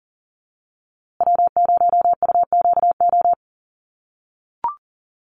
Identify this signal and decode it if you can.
Morse code, heard as a keyed tone, about 40 wpm, 700 Hz. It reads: W0UQO
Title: USA Call Signs Repeated 2x 40wpm